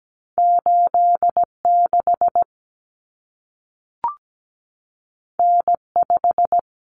Morse code, heard as a keyed tone, about 17 wpm, 700 Hz. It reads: 86 N5
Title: Common Cut Numbers - 2-Digit 17wpm